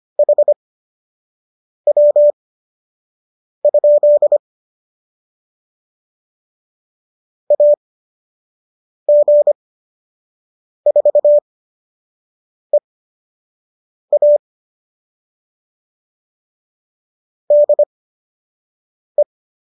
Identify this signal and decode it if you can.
Morse code, heard as a keyed tone, about 25 wpm, spaced as for 6 wpm, 590 Hz. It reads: HW? AG4EA DE